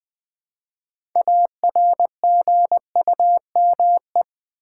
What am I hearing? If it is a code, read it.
Morse code, heard as a keyed tone, about 20 wpm, 700 Hz. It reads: ARGUME
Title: Build A Word - 2 Letters At A Time - 8-Letter Words - 20wpm